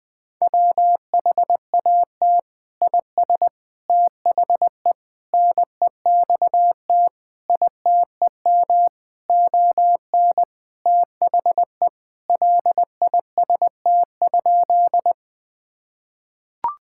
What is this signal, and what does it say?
Morse code, heard as a keyed tone, about 20 wpm, 700 Hz. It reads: WHAT IS THE NEXT ITEM ON THE LIST?
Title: Sentences from Top 1000 Words - Review Part 6 - 20wpm